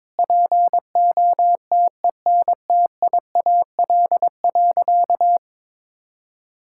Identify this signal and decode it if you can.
Morse code, heard as a keyed tone, about 22 wpm, 700 Hz. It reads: POTENTIAL.
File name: Sentences with 9-Word Phrases - 22wpm